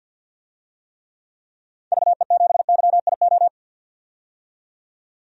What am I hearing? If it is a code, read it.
Morse code, heard as a keyed tone, about 50 wpm, 700 Hz. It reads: VE7YIO
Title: DXCC-100 Call Signs - 6 Characters - 50wpm